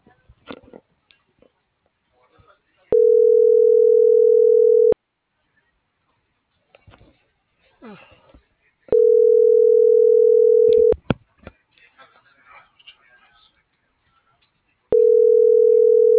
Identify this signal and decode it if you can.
Ringback tone (North America), 440 + 480 Hz.